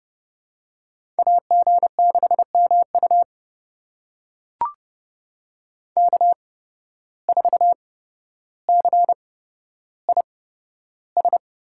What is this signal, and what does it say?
Morse code, heard as a keyed tone, about 30 wpm, spaced as for 13 wpm, 700 Hz. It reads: AG6MU K4CIS
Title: USA Call Signs - 5 Characters - 8x Character Spacing - 30wpm